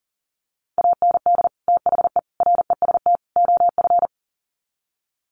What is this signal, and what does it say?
Morse code, heard as a keyed tone, about 40 wpm, 700 Hz. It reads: AND THE REST OF
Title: Sets of 4 Words 40wpm